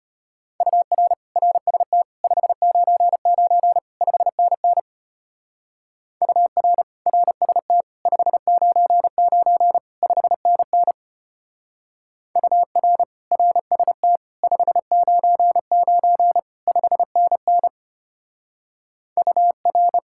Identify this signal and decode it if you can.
Morse code, heard as a keyed tone, about 32 wpm, 700 Hz. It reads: UR RST 599 5NN UR RST 599 5NN UR RST 599 5NN UR